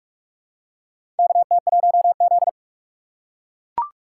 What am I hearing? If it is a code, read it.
Morse code, heard as a keyed tone, about 44 wpm, 700 Hz. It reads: KT1Z
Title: USA Call Signs - 4 Characters - 4x Character Spacing - 45wpm